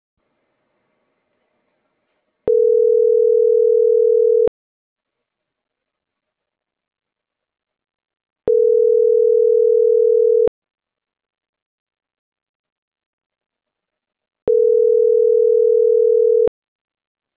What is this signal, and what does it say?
Ringback tone (North America), 440 + 480 Hz.